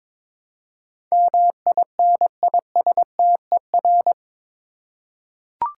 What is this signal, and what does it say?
Morse code, heard as a keyed tone, about 22 wpm, 700 Hz. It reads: MINISTER
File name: practice / 22wpm / Eight Letter Words - 22wpm